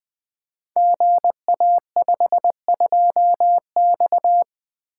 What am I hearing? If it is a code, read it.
Morse code, heard as a keyed tone, about 20 wpm, 700 Hz. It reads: GA52X
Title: Code Groups - 5 Letter:Numbers - 20wpm